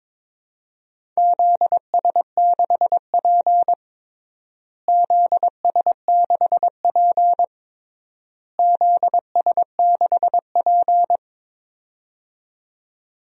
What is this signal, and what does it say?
Morse code, heard as a keyed tone, about 22 wpm, 700 Hz. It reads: ZS6P ZS6P ZS6P
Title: DXCC-100 Call Signs - 4 Characters - Repeated 3x - 22wpm